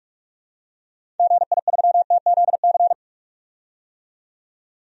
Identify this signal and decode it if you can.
Morse code, heard as a keyed tone, about 46 wpm, 700 Hz. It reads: GI3TZC
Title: DXCC-100 Call Signs - 6 Characters - 45wpm